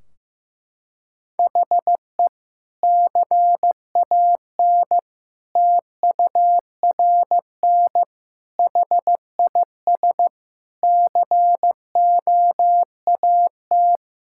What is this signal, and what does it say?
Morse code, heard as a keyed tone, about 15 wpm, 700 Hz. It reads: HE CAN TURN HIS COAT
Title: Sentences from Top 1600 Words 15wpm